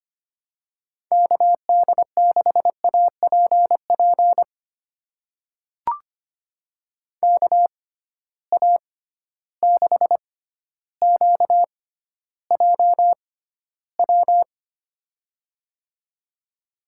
Morse code, heard as a keyed tone, about 25 wpm, spaced as for 13 wpm, 700 Hz. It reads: KD6APP KA6QJW
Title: USA Call Signs - 6 Characters - 6x Character Spacing - 25wpm